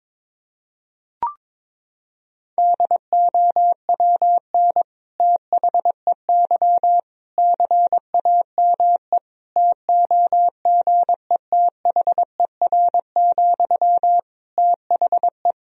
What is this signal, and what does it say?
Morse code, heard as a keyed tone, about 21 wpm, 700 Hz. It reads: DOWN THEY CAME TOGETHER, THE
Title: Sentences from Top 1000 Words - Review Part 2 - 22wpm